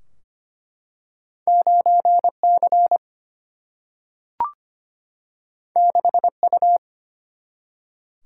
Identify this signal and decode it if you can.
Morse code, heard as a keyed tone, about 25 wpm, 700 Hz. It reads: 9C 6U
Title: Single-Letter Single-Number 25wpm